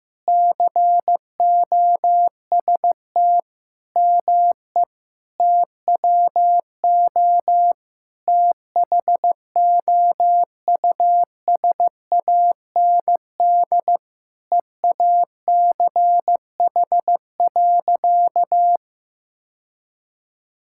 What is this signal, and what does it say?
Morse code, heard as a keyed tone, about 16 wpm, 700 Hz. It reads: COST ME TWO THOUSAND EACH.
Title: Sentences from Top 1000 Words - Review Part 5 - 15wpm